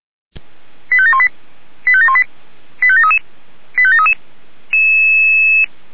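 Five-tone selective call (ZVEI), heard as quick runs of five tones, tones of about 70 ms. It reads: 85718 85718 85733 85733
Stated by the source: ZVEI five-tone selective call received on an SDR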